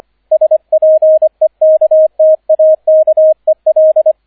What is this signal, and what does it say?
Morse code, heard as a keyed tone, about 25 wpm, 630 Hz, with radio noise. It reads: SPEKTAKEL